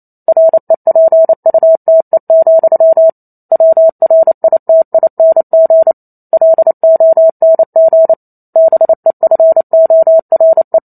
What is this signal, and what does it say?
Morse code, heard as a keyed tone, about 28 wpm, 650 Hz. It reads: REPUTE, WRITING LONG BEFORE